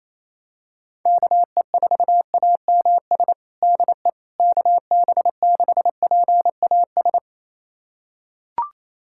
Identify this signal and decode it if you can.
Morse code, heard as a keyed tone, about 28 wpm, 700 Hz. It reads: KE4AMS DE KB6PAS
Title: USA Callsign DE Callsign 28wpm